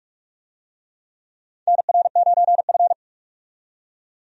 Morse code, heard as a keyed tone, about 41 wpm, 700 Hz. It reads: NR9F
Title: USA Call Signs - 4 Characters - 45wpm